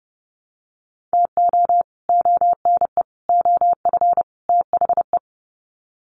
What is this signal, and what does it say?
Morse code, heard as a keyed tone, about 30 wpm, 700 Hz. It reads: TO ONE OF THE
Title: Sets of 4 Words 30wpm